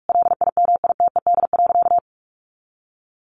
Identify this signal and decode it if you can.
Morse code, heard as a keyed tone, about 44 wpm, 700 Hz. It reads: LIMITED.
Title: Sentences from Top 900 Words 45wpm